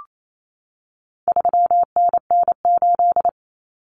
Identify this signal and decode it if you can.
Morse code, heard as a keyed tone, about 28 wpm, 700 Hz. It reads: 3NN8
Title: Common Cut Numbers - 4-Digit 28wpm